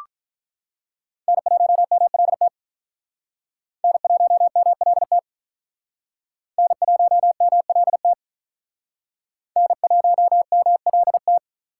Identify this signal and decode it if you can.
Morse code, heard as a keyed tone, about 43 wpm, 700 Hz. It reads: N1MLT N1MLT N1MLT N1MLT
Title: Speed Racing - USA 1x3 Call Signs 35wpm